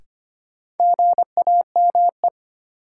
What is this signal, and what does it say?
Morse code, heard as a keyed tone, about 25 wpm, 700 Hz. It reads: GAME
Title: Sentences from Top 500 Words - New Word Drill - 25wpm